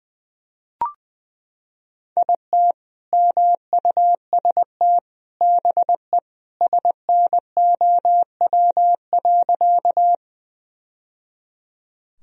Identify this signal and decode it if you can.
Morse code, heard as a keyed tone, about 20 wpm, 700 Hz. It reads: IT MUST BE SNOW.
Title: Sentences from Top 1000 Words - Review Part 3 - 20wpm